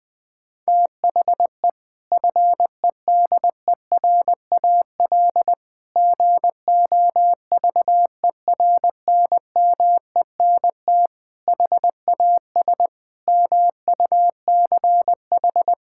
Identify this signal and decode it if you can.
Morse code, heard as a keyed tone, about 21 wpm, 700 Hz. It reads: THE FEDERAL GOVERNMENT HAS MUCH
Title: Sentences from Top 1000 Words - Review Part 5 - 20wpm